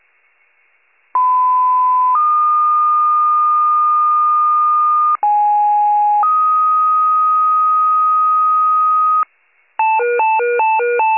Two-tone page, heard as a one-second tone, then a three-second tone, about 1010 then 1220 Hz, about 830 then 1250 Hz. It is heard from 1.1 to 9.2 s.
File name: wcalert